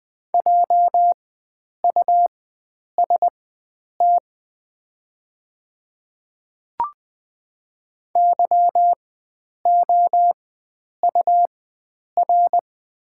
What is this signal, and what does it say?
Morse code, heard as a keyed tone, about 20 wpm, spaced as for 9 wpm, 700 Hz. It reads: JUST YOUR
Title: 4x Character Spacing - Top 300 Words - 20wpm